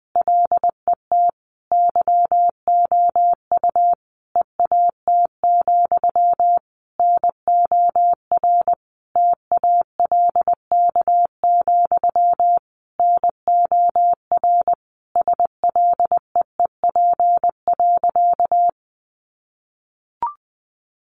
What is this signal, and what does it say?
Morse code, heard as a keyed tone, about 19 wpm, 700 Hz. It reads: LET YOU EAT, NOR TALK, NOR SLEEP.
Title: Sentences from Top 1500 Words 20wpm